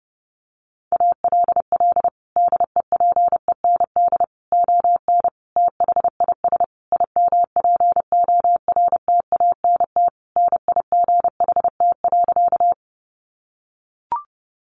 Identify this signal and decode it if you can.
Morse code, heard as a keyed tone, about 30 wpm, 700 Hz. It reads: ALL DEPEND ON THIS IMPORTANT NIGHT.
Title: Sentences from Top 1300 Words 30wpm